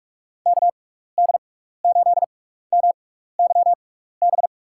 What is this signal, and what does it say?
Morse code, heard as a keyed tone, about 46 wpm, spaced as for 15 wpm, 700 Hz. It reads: KD8MYB
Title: USA Call Signs - 6x Character Spacing - 45wpm